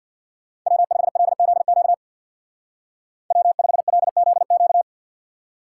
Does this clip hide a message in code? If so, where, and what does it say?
Yes: Morse code from 0.7 to 4.8 s, heard as a keyed tone, about 64 wpm, 700 Hz. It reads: W5LZQ W5LZQ